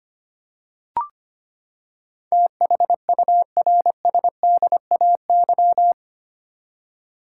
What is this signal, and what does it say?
Morse code, heard as a keyed tone, about 25 wpm, 700 Hz. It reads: THURSDAY
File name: Eight Letter Words - 25wpm